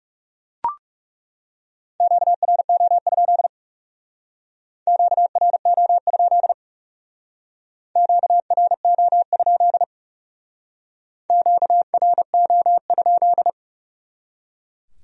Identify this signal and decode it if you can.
Morse code, heard as a keyed tone, about 38 wpm, 700 Hz. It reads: QRO? QRO? QRO? QRO?